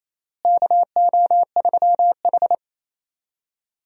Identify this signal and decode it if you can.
Morse code, heard as a keyed tone, about 27 wpm, 700 Hz. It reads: KO3H
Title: USA Call Signs - 4 Characters - Repeated 3x - 28wpm